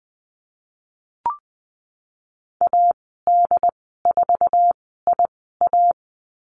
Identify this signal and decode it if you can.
Morse code, heard as a keyed tone, about 20 wpm, spaced as for 15 wpm, 700 Hz. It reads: AD4IA
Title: 069 Lesson 12g - Letter d - Call Signs 20wpm